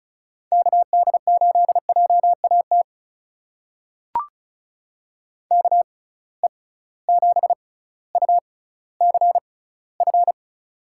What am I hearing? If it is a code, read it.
Morse code, heard as a keyed tone, about 36 wpm, spaced as for 18 wpm, 700 Hz. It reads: KD8JAT KE7UCF
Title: USA Call Signs - 6x Character Spacing - 35wpm